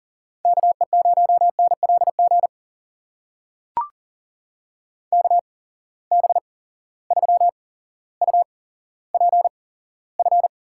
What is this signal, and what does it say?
Morse code, heard as a keyed tone, about 40 wpm, spaced as for 17 wpm, 700 Hz. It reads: KE0NLG KB3UPF
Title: USA Call Signs - 6 Characters - 8x Character Spacing - 40wpm